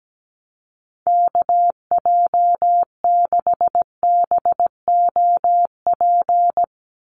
Morse code, heard as a keyed tone, about 17 wpm, 700 Hz. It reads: KJ6BOP
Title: USA Call Signs 17wpm